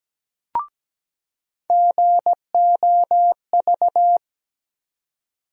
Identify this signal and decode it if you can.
Morse code, heard as a keyed tone, about 17 wpm, 700 Hz. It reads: GOV
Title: USA Call Sign Postfixes - 3 Letters 17wpm